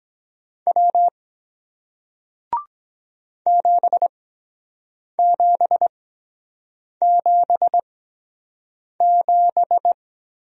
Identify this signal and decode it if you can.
Morse code, heard as a keyed tone, about 23 wpm, spaced as for 7 wpm, 700 Hz. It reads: W 7777